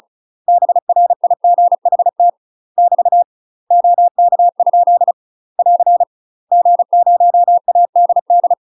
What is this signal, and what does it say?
Morse code, heard as a keyed tone, about 36 wpm, 700 Hz. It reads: BRIGHT <BT> OK? <AR> G0ADD
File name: UK MASTER QSOs TRAINING / 35WPM / UK MASTER QSOs 0038